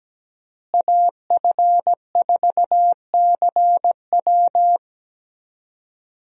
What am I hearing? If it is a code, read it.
Morse code, heard as a keyed tone, about 17 wpm, 700 Hz. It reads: AF4CW
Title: USA Call Signs - 5 Characters - Repeated 2x - 17wpm